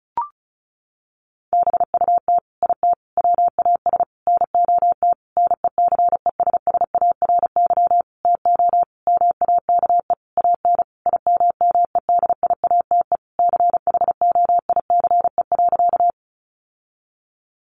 Morse code, heard as a keyed tone, about 34 wpm, 700 Hz. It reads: BUT IT WAS NOT NECESSARY TO MAKE AN IMMEDIATE CHOICE.
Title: Sentences from Top 1200 Words 35wpm